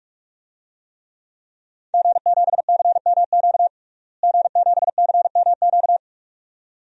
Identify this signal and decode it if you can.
Morse code, heard as a keyed tone, about 46 wpm, 700 Hz. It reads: G7CMQ G7CMQ
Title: DXCC-100 Call Signs - 5 Characters - Repeated 2x - 45wpm